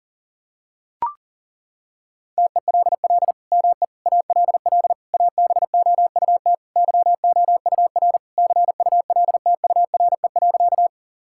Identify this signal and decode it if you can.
Morse code, heard as a keyed tone, about 37 wpm, 700 Hz. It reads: TELL ME ALL ABOUT YOUR CULTURE.
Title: Sentences from Top 1000 Words - Review Part 6 - 40wpm